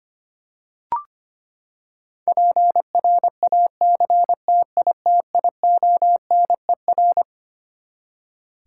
Morse code, heard as a keyed tone, about 25 wpm, 700 Hz. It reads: PRACTITIONER
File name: Twelve Letter Words - 25wpm